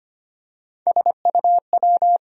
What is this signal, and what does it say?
Morse code, heard as a keyed tone, about 25 wpm, 700 Hz. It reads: SUW
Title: USA Call Sign Postfixes - 3 Letters 25wpm